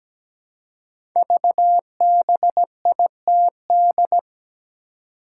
Morse code, heard as a keyed tone, about 17 wpm, 700 Hz. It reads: VBITD
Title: Code Groups - 5 Letters - 17wpm